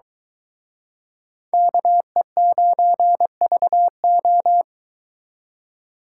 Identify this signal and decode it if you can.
Morse code, heard as a keyed tone, about 23 wpm, 700 Hz. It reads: KE9VO